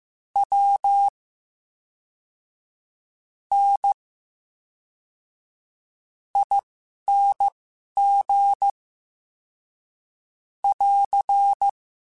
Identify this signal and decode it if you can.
Morse code, heard as a keyed tone, about 15 wpm, spaced as for 11 wpm, 800 Hz. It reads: W N ING <AR>
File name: Special INGs / Special INGs-mp3s / VerySlowToFaster Ing @10